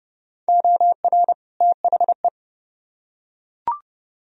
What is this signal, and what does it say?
Morse code, heard as a keyed tone, about 29 wpm, 700 Hz. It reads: OR THE